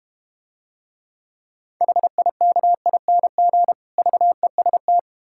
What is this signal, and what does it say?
Morse code, heard as a keyed tone, about 32 wpm, 690 Hz. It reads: HIKING VEST